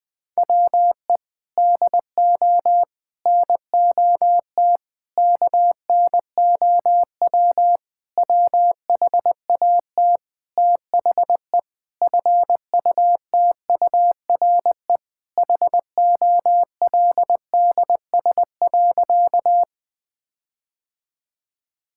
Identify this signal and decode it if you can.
Morse code, heard as a keyed tone, about 20 wpm, 700 Hz. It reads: WE DO NOT KNOW WHAT THE FUTURE HOLDS.
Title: Sentences with 5-Word Phrases - 20wpm